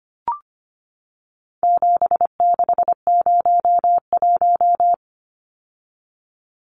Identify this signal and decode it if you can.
Morse code, heard as a keyed tone, about 25 wpm, 700 Hz. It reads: 7601
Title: Numbers - 4-Digit 25wpm